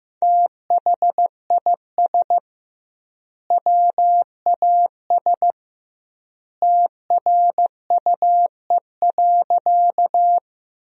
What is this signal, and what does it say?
Morse code, heard as a keyed tone, about 15 wpm, 700 Hz. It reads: THIS WAS TRUE.
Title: 2x Word-Spacing - Sentences from Top 400 Words - 15wpm